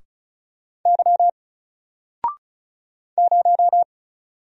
Morse code, heard as a keyed tone, about 36 wpm, 700 Hz. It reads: Y 0
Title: Single Letter-Number Alt 35wpm